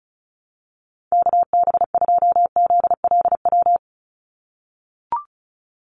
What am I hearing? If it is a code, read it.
Morse code, heard as a keyed tone, about 34 wpm, 700 Hz. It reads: KB2ZLW
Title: USA Call Signs 35wpm